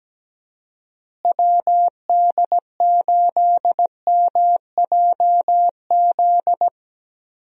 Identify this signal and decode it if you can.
Morse code, heard as a keyed tone, about 17 wpm, 700 Hz. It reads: WD8MJZ
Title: USA Call Signs - 6 Characters - Repeated 3x - 17wpm